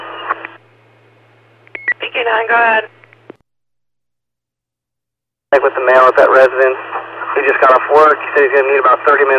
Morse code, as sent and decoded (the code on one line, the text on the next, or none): none